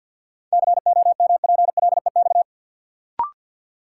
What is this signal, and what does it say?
Morse code, heard as a keyed tone, about 52 wpm, 700 Hz. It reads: COMPLEX
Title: Sentences from Top 900 Words - New Word Drill - 50wpm